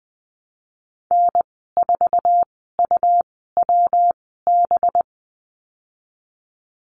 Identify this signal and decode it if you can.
Morse code, heard as a keyed tone, about 20 wpm, spaced as for 15 wpm, 700 Hz. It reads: N4UWB
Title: 193 Lesson 31g - Letter v - Call Signs 20wpm